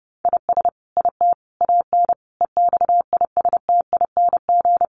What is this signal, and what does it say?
Morse code, heard as a keyed tone, about 30 wpm, 700 Hz. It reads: IS IT AN EXISTING
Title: Sentences from Top 1500 Words 30wpm